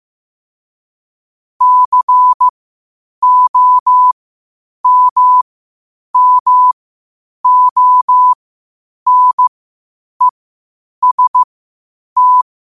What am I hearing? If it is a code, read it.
Morse code, heard as a keyed tone, about 15 wpm, spaced as for 8 wpm, 1000 Hz. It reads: COMMONEST
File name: qotdfiles / 8wpm